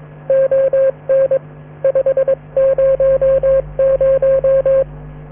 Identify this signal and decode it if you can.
Morse code, heard as a keyed tone, about 22 wpm, 540 Hz, with radio noise. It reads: ON 500